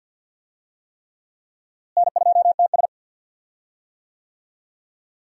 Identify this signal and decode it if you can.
Morse code, heard as a keyed tone, about 51 wpm, 700 Hz. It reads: N2TS
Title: USA Call Signs - 4 Characters - Repeated 3x - 50wpm